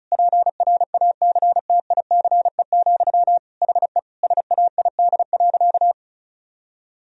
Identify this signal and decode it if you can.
Morse code, heard as a keyed tone, about 36 wpm, 700 Hz. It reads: PRACTICE, HE SAID.
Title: Sentences from Top 1000 Words - Review Part 4 - 35wpm